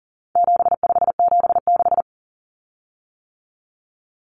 Morse code, heard as a keyed tone, about 40 wpm, 700 Hz. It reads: 7576
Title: Common Cut Numbers - 4-Digit 40wpm